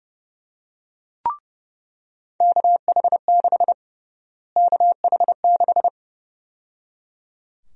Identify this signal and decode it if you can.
Morse code, heard as a keyed tone, about 30 wpm, 700 Hz. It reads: KH6 KH6
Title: USA Call Sign Prefixes - Repeated 2x - 30wpm